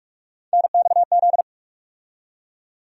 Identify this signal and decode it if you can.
Morse code, heard as a keyed tone, about 46 wpm, 700 Hz. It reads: NXZ
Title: USA Call Sign Postfixes - 3 Letters 45wpm